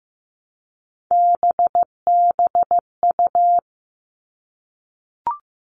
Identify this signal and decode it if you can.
Morse code, heard as a keyed tone, about 15 wpm, 700 Hz. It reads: BBU
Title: All Cut Numbers - 3-Digit 15wpm